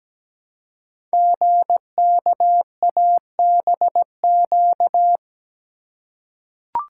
Morse code, heard as a keyed tone, about 17 wpm, 700 Hz. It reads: GKABQ
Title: Code Groups - 5 Letters - 17wpm